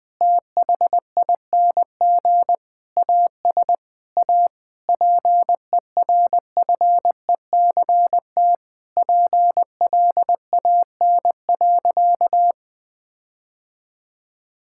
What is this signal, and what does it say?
Morse code, heard as a keyed tone, about 20 wpm, 700 Hz. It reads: THING AS A PERFECT PLAN.
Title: Sentences with 7-Word Phrases - 20wpm